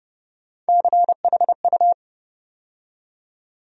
Morse code, heard as a keyed tone, about 32 wpm, 700 Hz. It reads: CHU
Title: USA Call Sign Postfixes - 3 Letters 30wpm